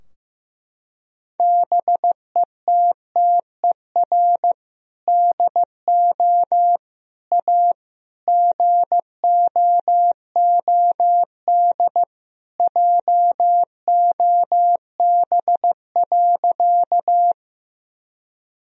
Morse code, heard as a keyed tone, about 15 wpm, 700 Hz. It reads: BETTER DO A GOOD JOB.
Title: Sentences from Top 1000 Words - Review Part 3 - 15wpm